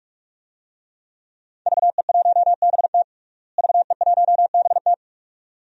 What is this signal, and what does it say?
Morse code, heard as a keyed tone, about 42 wpm, 700 Hz. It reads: VE1BT VE1BT
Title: DXCC-100 Call Signs - 5 Characters - Repeated 2x - 45wpm